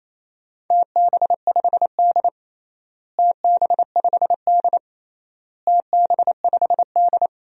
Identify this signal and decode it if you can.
Morse code, heard as a keyed tone, about 28 wpm, 700 Hz. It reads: TB5D TB5D TB5D
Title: DXCC-100 Call Signs - 4 Characters - Repeated 3x - 28wpm